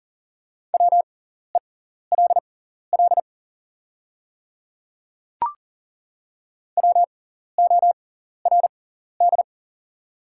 Morse code, heard as a keyed tone, about 40 wpm, spaced as for 14 wpm, 690 Hz. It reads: WELL WORD
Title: 6x Character Spacing - Top 200 Words - 40wpm